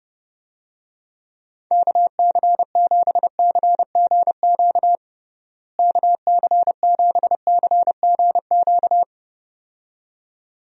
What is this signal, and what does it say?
Morse code, heard as a keyed tone, about 32 wpm, 700 Hz. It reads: KC7CGQ KC7CGQ
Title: USA Call Signs - 6 Characters - Repeated 2x - 30wpm